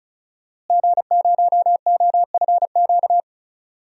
Morse code, heard as a keyed tone, about 36 wpm, 700 Hz. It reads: G0OFQ
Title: DXCC-100 Call Signs - 5 Characters - Repeated 2x - 35wpm